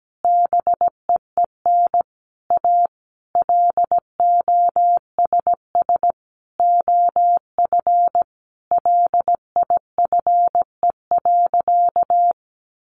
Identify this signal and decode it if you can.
Morse code, heard as a keyed tone, about 17 wpm, 700 Hz. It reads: BEEN A LOSS OF LIFE.
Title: Sentences from Top 1400 Words 17wpm